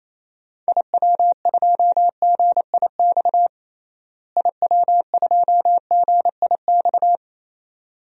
Morse code, heard as a keyed tone, about 28 wpm, 700 Hz. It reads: IW2GIX IW2GIX
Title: DXCC-100 Call Signs - 6 Characters - Repeated 2x - 28wpm